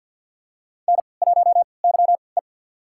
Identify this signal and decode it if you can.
Morse code, heard as a keyed tone, about 53 wpm, spaced as for 29 wpm, 700 Hz. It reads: N1YE
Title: USA Call Signs - 4 Characters - 3x Character Spacing - 50wpm